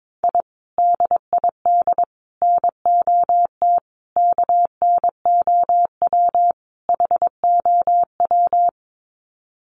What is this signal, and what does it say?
Morse code, heard as a keyed tone, about 22 wpm, 700 Hz. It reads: I DID NOT KNOW HOW
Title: Sets of 5 Words Encore 22wpm